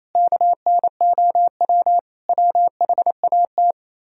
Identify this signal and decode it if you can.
Morse code, heard as a keyed tone, about 36 wpm, 700 Hz. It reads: KNOW WHAT